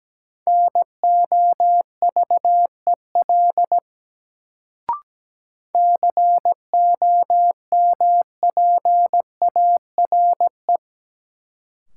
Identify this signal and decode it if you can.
Morse code, heard as a keyed tone, about 17 wpm, 700 Hz. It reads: NOVEL COMPARE
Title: Sentences from Top 1300 Words - New Word Drill - 17wpm